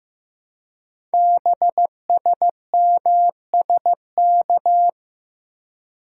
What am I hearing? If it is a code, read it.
Morse code, heard as a keyed tone, about 15 wpm, 700 Hz. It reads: BSMSK
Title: Code Groups - 5 Letters - 15wpm